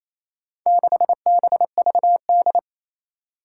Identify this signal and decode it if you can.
Morse code, heard as a keyed tone, about 27 wpm, 700 Hz. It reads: 6BVD
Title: Four Challenging Letter-Number Combinations 28wpm